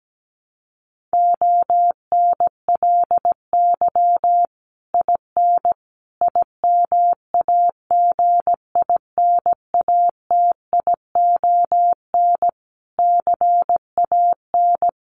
Morse code, heard as a keyed tone, about 17 wpm, 700 Hz. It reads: ONLY IN IMAGINATION CAN
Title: Sentences from Top 1300 Words 17wpm